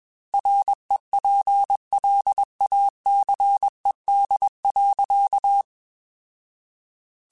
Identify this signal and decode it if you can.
Morse code, heard as a keyed tone, about 21 wpm, 800 Hz. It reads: REPLACED.
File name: Chuckles / Chuckles-mp3s / Squeaky @20